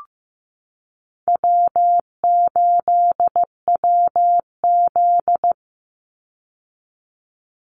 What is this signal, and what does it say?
Morse code, heard as a keyed tone, about 15 wpm, 700 Hz. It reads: W8WZ
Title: Easy Call Signs Repeated 3x - 15wpm